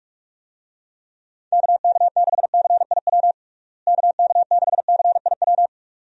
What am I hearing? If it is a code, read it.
Morse code, heard as a keyed tone, about 46 wpm, 700 Hz. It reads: KK6CIW KK6CIW
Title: USA Call Signs - 6 Characters - Repeated 2x - 45wpm